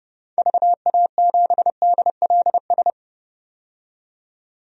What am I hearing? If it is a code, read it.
Morse code, heard as a keyed tone, about 30 wpm, 700 Hz. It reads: VA7DLS
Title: DXCC-100 Call Signs - 6 Characters - 30wpm